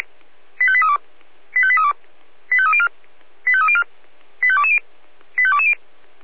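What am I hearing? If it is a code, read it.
Five-tone selective call (ZVEI), heard as quick runs of five tones, tones of about 70 ms. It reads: 85832 85832 85304 85304 85220 85220
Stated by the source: ZVEI five-tone selective call received on an SDR.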